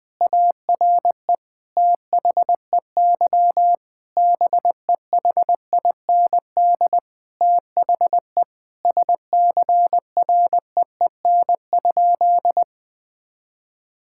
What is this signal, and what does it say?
Morse code, heard as a keyed tone, about 21 wpm, 700 Hz. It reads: ARE THEY BEHIND THE SCREEN?
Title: Sentences from Top 1000 Words - Review Part 6 - 20wpm